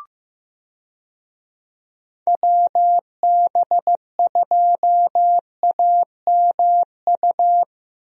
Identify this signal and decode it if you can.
Morse code, heard as a keyed tone, about 15 wpm, 700 Hz. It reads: WB2AMU
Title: USA Call Signs - 6 Characters - 15wpm